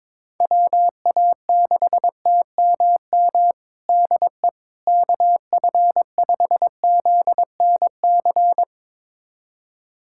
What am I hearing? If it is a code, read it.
Morse code, heard as a keyed tone, about 22 wpm, 700 Hz. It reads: WA6TMM DE KF5ZNC
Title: USA Callsign DE Callsign - 6 Characters - 22wpm